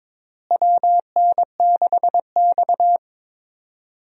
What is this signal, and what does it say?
Morse code, heard as a keyed tone, about 22 wpm, 700 Hz. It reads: WN6X
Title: USA Call Signs - 4 Characters - Repeated 3x - 22wpm